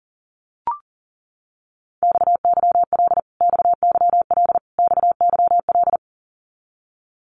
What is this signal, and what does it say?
Morse code, heard as a keyed tone, about 40 wpm, 700 Hz. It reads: XYL XYL XYL
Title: Abbreviations Repeated 3x - 40wpm